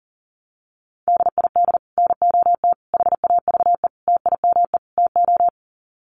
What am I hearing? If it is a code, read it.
Morse code, heard as a keyed tone, about 40 wpm, 700 Hz. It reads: DID NOT HAVE TIME TO
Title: Sets of 5 Words Encore 40wpm